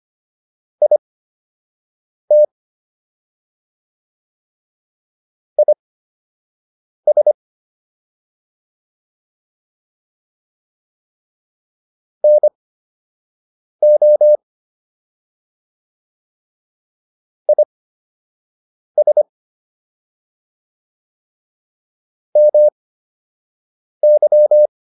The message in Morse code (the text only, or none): IT IS NO IS MY